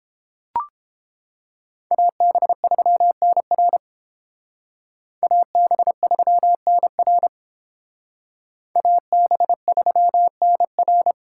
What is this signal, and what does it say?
Morse code, heard as a keyed tone, about 28 wpm, 700 Hz. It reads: AB3NR AB3NR AB3NR